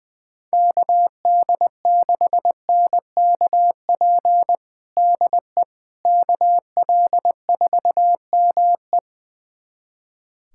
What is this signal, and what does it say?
Morse code, heard as a keyed tone, about 20 wpm, 700 Hz. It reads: KD6NKP DE KL4ME
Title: USA Callsign DE Callsign 20wpm